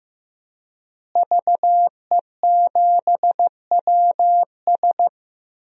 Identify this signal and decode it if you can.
Morse code, heard as a keyed tone, about 15 wpm, 700 Hz. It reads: VE7WS
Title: DXCC-100 Call Signs - 5 Characters - Repeated 2x - 15wpm